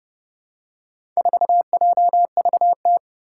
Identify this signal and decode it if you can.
Morse code, heard as a keyed tone, about 30 wpm, 700 Hz. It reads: 4JVT
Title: Four Challenging Letter-Number Combinations 30wpm